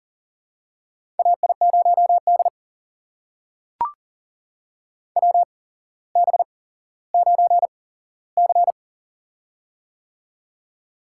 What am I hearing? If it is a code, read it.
Morse code, heard as a keyed tone, about 41 wpm, spaced as for 17 wpm, 700 Hz. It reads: AI0D WB9C
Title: USA Call Signs - 4 Characters - 8x Character Spacing - 40wpm